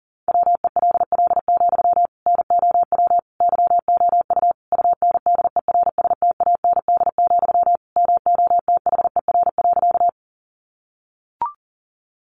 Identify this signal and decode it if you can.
Morse code, heard as a keyed tone, about 40 wpm, 700 Hz. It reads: WELL, NOW YOU UNDERSTAND, MOTHER.
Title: Sentences from Top 600 Words 40wpm